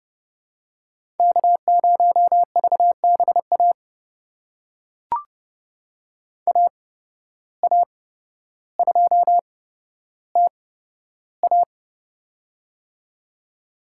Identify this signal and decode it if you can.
Morse code, heard as a keyed tone, about 30 wpm, spaced as for 13 wpm, 700 Hz. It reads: K0VBA AA2TA
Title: USA Call Signs - 5 Characters - 8x Character Spacing - 30wpm